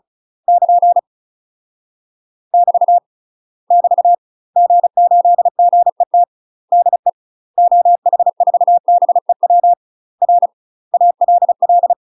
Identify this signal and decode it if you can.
Morse code, heard as a keyed tone, about 34 wpm, 700 Hz. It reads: <KN> <BT> <BT> G8GET DE OH4BEW R ALL
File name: UK MASTER QSOs TRAINING / 35WPM / UK MASTER QSOs 0037